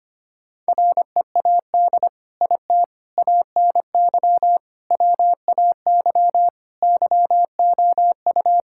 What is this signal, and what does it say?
Morse code, heard as a keyed tone, about 25 wpm, 700 Hz. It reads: READ IT ANY WAY YOU
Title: Sentences from Top 1600 Words 25wpm